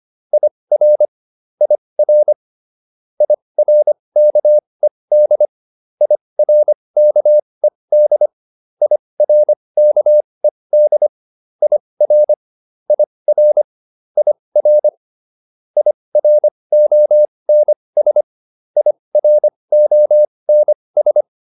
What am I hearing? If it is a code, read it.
Morse code, heard as a keyed tone, about 25 wpm, spaced as for 20 wpm, 600 Hz. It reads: IR IR IRKED IRKED IRKED IR IR IR IRONS IRONS